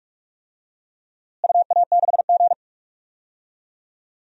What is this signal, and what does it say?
Morse code, heard as a keyed tone, about 46 wpm, 700 Hz. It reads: UA6G
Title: DXCC-100 Call Signs - 4 Characters - Repeated 2x - 45wpm